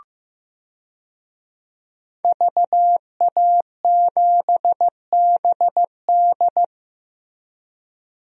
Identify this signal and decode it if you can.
Morse code, heard as a keyed tone, about 15 wpm, 700 Hz. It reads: VA7BD
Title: DXCC-100 Call Signs - 5 Characters - Repeated 2x - 15wpm